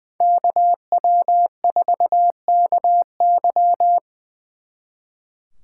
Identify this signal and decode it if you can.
Morse code, heard as a keyed tone, about 23 wpm, 700 Hz. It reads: KW4KY
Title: Speed Racing - USA Call Signs 20wpm